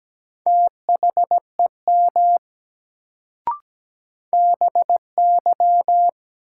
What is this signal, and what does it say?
Morse code, heard as a keyed tone, about 17 wpm, 700 Hz. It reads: THEM BY
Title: Sentences from Top 100 Words - New Word Drill - 17wpm